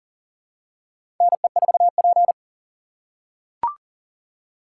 Morse code, heard as a keyed tone, about 40 wpm, 700 Hz. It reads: NE4P
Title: USA Call Signs - 4 Characters - 3x Character Spacing - 40wpm